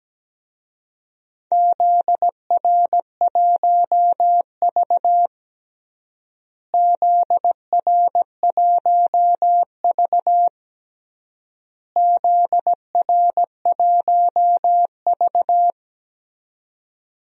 Morse code, heard as a keyed tone, about 17 wpm, 700 Hz. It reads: ZR1V ZR1V ZR1V
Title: DXCC-100 Call Signs - 4 Characters - Repeated 3x - 17wpm